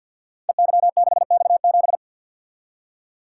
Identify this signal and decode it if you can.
Morse code, heard as a keyed tone, about 48 wpm, 700 Hz. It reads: EY6X7
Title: Code Groups - 5 Letter:Numbers - 50wpm